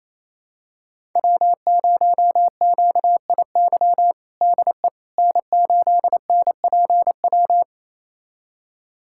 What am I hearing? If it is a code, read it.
Morse code, heard as a keyed tone, about 28 wpm, 700 Hz. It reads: W0QIY DE N8NPW
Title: USA Callsign DE Callsign - 5 Characters - 28wpm